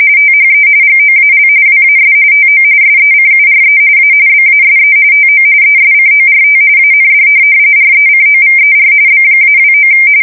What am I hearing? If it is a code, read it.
RTTY, about 45.45 baud. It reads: THE GOAL WAS TO SECURE A BASE TO ESCORT AND RECOVER B-29
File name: fsk170 / USS-New-Jersey